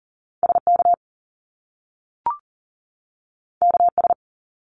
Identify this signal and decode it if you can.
Morse code, heard as a keyed tone, about 42 wpm, 700 Hz. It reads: SK KS
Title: Top Two Letter Combinations in Words 40wpm